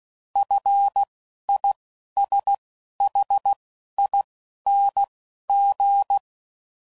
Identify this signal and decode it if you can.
Morse code, heard as a keyed tone, about 16 wpm, spaced as for 11 wpm, 800 Hz. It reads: FISHING
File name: Prefixes / FLs & THs / F Prefixes